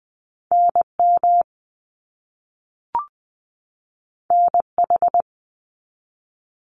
Morse code, heard as a keyed tone, about 20 wpm, 700 Hz. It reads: NM NH
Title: US States 20wpm